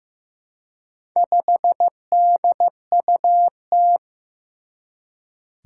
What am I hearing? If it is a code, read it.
Morse code, heard as a keyed tone, about 15 wpm, 700 Hz. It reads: 5DUT
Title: Four Challenging Letter-Number Combinations 15wpm